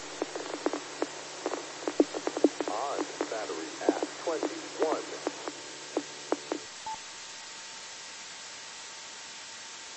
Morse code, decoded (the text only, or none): none